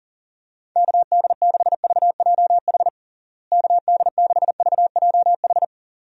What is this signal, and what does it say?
Morse code, heard as a keyed tone, about 40 wpm, 700 Hz. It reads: KD6VJH KD6VJH
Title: USA Call Signs - 6 Characters - Repeated 2x - 40wpm